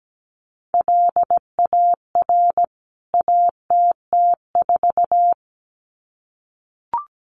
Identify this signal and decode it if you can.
Morse code, heard as a keyed tone, about 17 wpm, 700 Hz. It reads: LAR ATT4
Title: CWT Exchange 17wpm